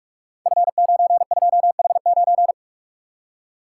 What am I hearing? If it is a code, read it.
Morse code, heard as a keyed tone, about 44 wpm, 700 Hz. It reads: F92H9
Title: Code Groups - 5 Letter:Numbers - 45wpm